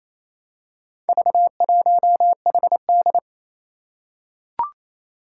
Morse code, heard as a keyed tone, about 27 wpm, 700 Hz. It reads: V1HD
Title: Four Challenging Letter-Number Combinations 28wpm